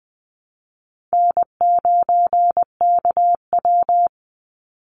Morse code, heard as a keyed tone, about 20 wpm, 700 Hz. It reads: N9KW
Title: NA SOTA Chaser Callsigns 20wpm